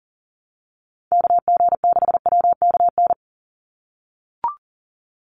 Morse code, heard as a keyed tone, about 40 wpm, 700 Hz. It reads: KG6WKN
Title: USA Call Signs 40wpm